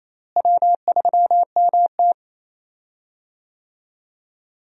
Morse code, heard as a keyed tone, about 28 wpm, 700 Hz. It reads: W3MT
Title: USA Call Signs - 4 Characters - Repeated 3x - 28wpm